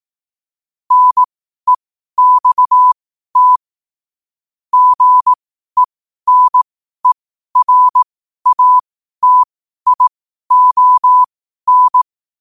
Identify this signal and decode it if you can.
Morse code, heard as a keyed tone, about 18 wpm, spaced as for 12 wpm, 1000 Hz. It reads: NEXT GENERATION
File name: qotdfiles / 12wpm